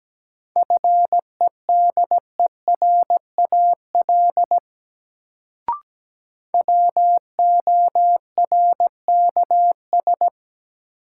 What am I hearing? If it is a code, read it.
Morse code, heard as a keyed tone, about 17 wpm, 700 Hz. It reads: FEDERAL WORKS
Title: Sentences from Top 1000 Words - Review Part 5 - New Word Drill - 17wpm